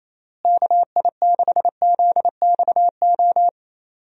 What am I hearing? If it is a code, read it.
Morse code, heard as a keyed tone, about 28 wpm, 700 Hz. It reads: KI6ZXO
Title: USA Call Signs - 6 Characters - Repeated 3x - 28wpm